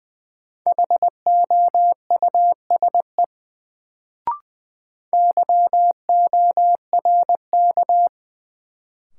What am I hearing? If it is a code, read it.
Morse code, heard as a keyed tone, about 20 wpm, 700 Hz. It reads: HOUSE YORK
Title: Sets of 2 Words - New Word Drill - 20wpm